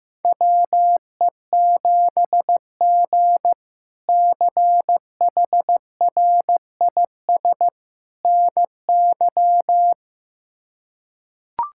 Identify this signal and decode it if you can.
Morse code, heard as a keyed tone, about 15 wpm, 700 Hz. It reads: WE7G CHRIS NY
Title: K1USN SST QSO Practice - 15wpm